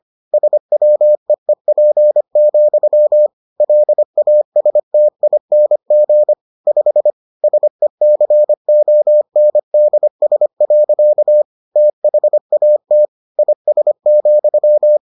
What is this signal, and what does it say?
Morse code, heard as a keyed tone, about 25 wpm, 600 Hz. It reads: SWEEP, LASTING 5 SECONDS. THAT IS,